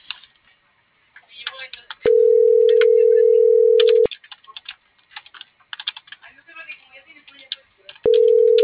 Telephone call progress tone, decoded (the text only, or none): ringback tone